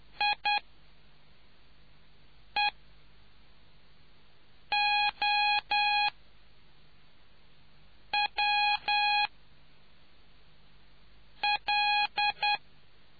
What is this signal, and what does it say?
Morse code, heard as a keyed tone, about 10 wpm, spaced as for 4 wpm, 830 Hz. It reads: IEOWL